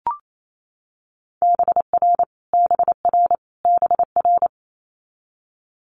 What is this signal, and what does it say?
Morse code, heard as a keyed tone, about 28 wpm, 700 Hz. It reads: BR BR BR
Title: Multi-letter Phonemes Repeated 3x 28wpm